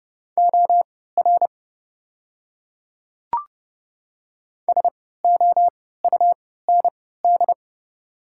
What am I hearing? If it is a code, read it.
Morse code, heard as a keyed tone, about 30 wpm, spaced as for 17 wpm, 690 Hz. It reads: OR SOUND
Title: 3x Character Spacing - Top 100 Words - 30wpm